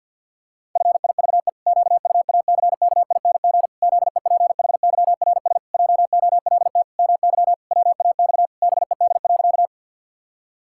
Morse code, heard as a keyed tone, about 51 wpm, 700 Hz. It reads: FIVE QUACKING ZEPHYRS JOLT MY WAX BED.
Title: Pangram Challenge v2 - 50wpm